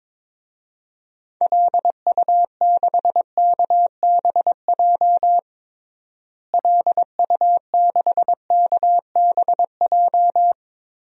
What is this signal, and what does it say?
Morse code, heard as a keyed tone, about 22 wpm, 700 Hz. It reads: LU6KBJ LU6KBJ
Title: DXCC-100 Call Signs - 6 Characters - Repeated 3x - 22wpm